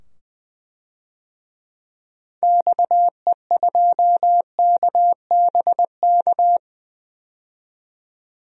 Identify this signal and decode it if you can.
Morse code, heard as a keyed tone, about 20 wpm, 700 Hz. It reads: XE2KBK
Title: DXCC-100 Call Signs - 6 Characters - 20wpm